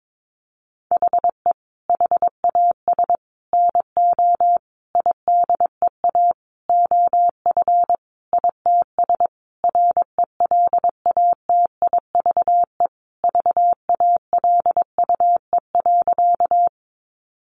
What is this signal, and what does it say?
Morse code, heard as a keyed tone, about 22 wpm, 700 Hz. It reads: HE HAS NO IDEA OF ITS RELATIVE VALUE.
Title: Sentences from Top 1400 Words 22wpm